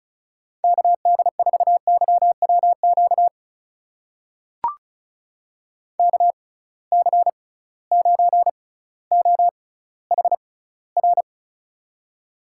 Morse code, heard as a keyed tone, about 34 wpm, spaced as for 18 wpm, 700 Hz. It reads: KD4YWQ KC9OHR
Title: USA Call Signs - 6 Characters - 6x Character Spacing - 35wpm